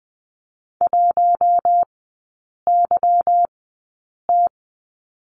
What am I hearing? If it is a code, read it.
Morse code, heard as a keyed tone, about 20 wpm, spaced as for 8 wpm, 690 Hz. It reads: 1YT
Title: 145 Lesson 24e - Letter y - Three Characters 20wpm